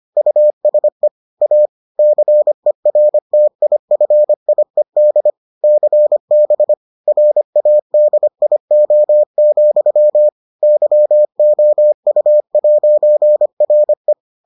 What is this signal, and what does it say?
Morse code, heard as a keyed tone, about 25 wpm, 600 Hz. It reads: USE A CERTIFIED CB RADIO, YOU'RE